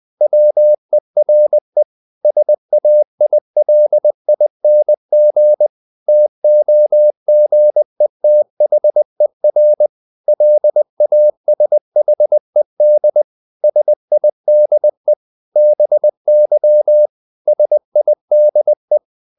Morse code, heard as a keyed tone, about 20 wpm, 600 Hz. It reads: WERE SAILING TOGETHER LASHED SIDE BY SIDE
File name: the old man and the sea part 2 20wpm